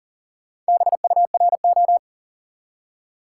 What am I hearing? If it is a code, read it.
Morse code, heard as a keyed tone, about 40 wpm, 700 Hz. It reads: BURO
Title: Top 90 QSO Elements 40wpm